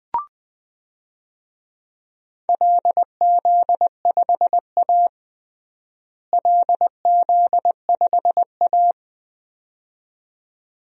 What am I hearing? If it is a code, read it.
Morse code, heard as a keyed tone, about 20 wpm, 700 Hz. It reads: LZ5A LZ5A
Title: DXCC-100 Call Signs - 4 Characters - Repeated 2x - 20wpm